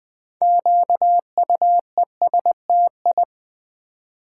Morse code, heard as a keyed tone, about 20 wpm, 700 Hz. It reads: QUESTI
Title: Build A Word - 2 Letters At A Time - 8-Letter Words - 20wpm